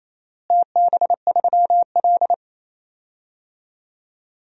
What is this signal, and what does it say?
Morse code, heard as a keyed tone, about 28 wpm, 700 Hz. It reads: TB3L
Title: DXCC-100 Call Signs - 4 Characters - Repeated 3x - 28wpm